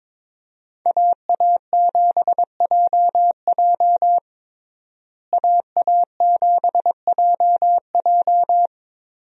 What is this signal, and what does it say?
Morse code, heard as a keyed tone, about 23 wpm, 700 Hz. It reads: AA7JJ AA7JJ
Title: USA Call Signs - 5 Characters - Repeated 3x - 22wpm